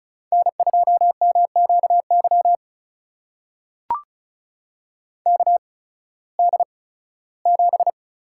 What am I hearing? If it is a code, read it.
Morse code, heard as a keyed tone, about 36 wpm, spaced as for 19 wpm, 700 Hz. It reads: N2MQY KD7
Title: USA Call Signs - 5 Characters - 8x Character Spacing - 35wpm